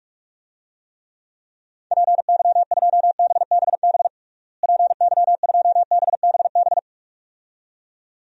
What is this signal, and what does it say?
Morse code, heard as a keyed tone, about 46 wpm, 700 Hz. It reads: PY2BBB PY2BBB
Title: DXCC-100 Call Signs - 6 Characters - Repeated 2x - 45wpm